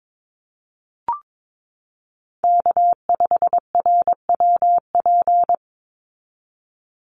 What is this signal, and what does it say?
Morse code, heard as a keyed tone, about 22 wpm, 700 Hz. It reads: K5RWP
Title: NA SOTA Chaser Callsigns 22wpm